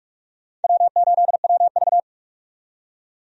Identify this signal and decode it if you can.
Morse code, heard as a keyed tone, about 46 wpm, 700 Hz. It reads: W8WV
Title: USA Call Signs - 4 Characters - 45wpm